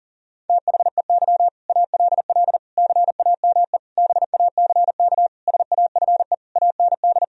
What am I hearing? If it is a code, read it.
Morse code, heard as a keyed tone, about 40 wpm, 700 Hz. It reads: THEY ALL CAME BACK SAFE AND
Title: Sentences from Top 1000 Words - Review Part 1 - 40wpm